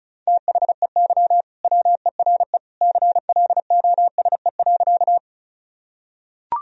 Morse code, heard as a keyed tone, about 36 wpm, 700 Hz. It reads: THEY WERE CLOSE.
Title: Sentences from Top 1000 Words - Review Part 1 - 35wpm